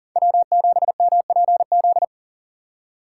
Morse code, heard as a keyed tone, about 40 wpm, 700 Hz. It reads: W7MPZ
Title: USA Call Signs - 5 Characters - 4x Character Spacing - 40wpm